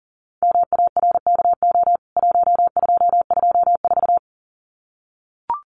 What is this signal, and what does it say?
Morse code, heard as a keyed tone, about 40 wpm, 700 Hz. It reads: MARKO 1224
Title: CWT Exchange 40wpm